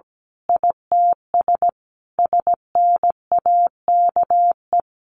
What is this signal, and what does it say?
Morse code, heard as a keyed tone, about 17 wpm, 700 Hz. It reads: ITS SNAKE
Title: A Sound Of Thunder 17wpm